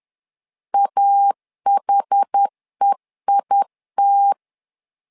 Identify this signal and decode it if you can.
Morse code, heard as a keyed tone, about 10 wpm, 800 Hz. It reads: AHEIT
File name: LEZIONI / 2 UHDN / 2 UH 10wpm 1714